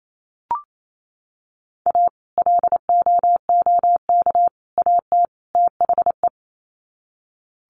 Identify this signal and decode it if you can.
Morse code, heard as a keyed tone, about 28 wpm, 700 Hz. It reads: A LOOK AT THE
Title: Sets of 4 Words 28wpm